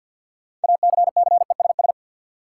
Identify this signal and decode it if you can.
Morse code, heard as a keyed tone, about 52 wpm, 700 Hz. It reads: ACCESS
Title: Sentences from Top 1000 Words - Review Part 4 - New Word Drill - 50wpm